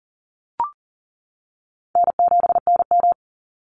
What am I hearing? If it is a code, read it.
Morse code, heard as a keyed tone, about 40 wpm, 700 Hz. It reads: N7NM
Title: Easy Call Signs - 40wpm